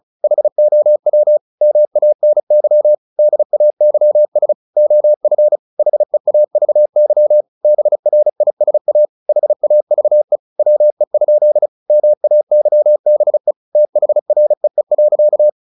Morse code, heard as a keyed tone, about 35 wpm, 600 Hz. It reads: HOW MANY DAYS OF HEAVY BRISA HAVE WE? MAYBE THREE.
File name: the old man and the sea 2 35wpm